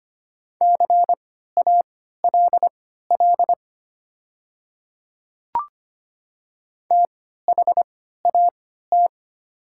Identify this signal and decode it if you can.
Morse code, heard as a keyed tone, about 25 wpm, spaced as for 14 wpm, 700 Hz. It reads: CALL THAT
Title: 3x Character Spacing - Top 200 Words - 25wpm